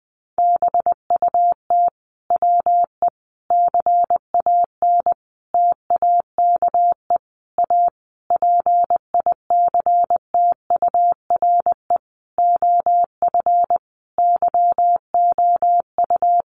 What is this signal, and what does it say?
Morse code, heard as a keyed tone, about 20 wpm, 700 Hz. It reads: BUT WE CAN TAKE A PICTURE OF YOU
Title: A Sound Of Thunder 20wpm